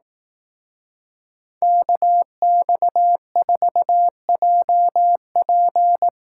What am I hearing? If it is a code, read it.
Morse code, heard as a keyed tone, about 18 wpm, 700 Hz. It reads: KX4JP
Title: Speed Racing - USA Call Signs 15wpm